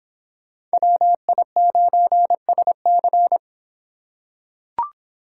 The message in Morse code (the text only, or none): WI9SC